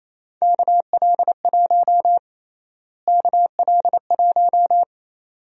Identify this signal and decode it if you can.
Morse code, heard as a keyed tone, about 28 wpm, 700 Hz. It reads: KL1 KL1
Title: USA Call Sign Prefixes - Repeated 2x - 28wpm